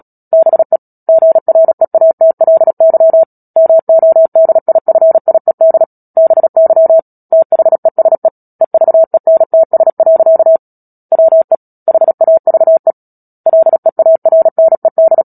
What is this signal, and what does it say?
Morse code, heard as a keyed tone, about 37 wpm, 650 Hz. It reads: BE GREATLY MODIFIED BY THESE EVENTS. WE HAVE LEARNED
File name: Book Two / 10-Chapter 10 - Epilogue - 36 WPM